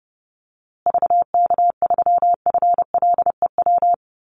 Morse code, heard as a keyed tone, about 30 wpm, 700 Hz. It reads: VK3FLEW
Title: USA Call Signs Repeated 3x 30wpm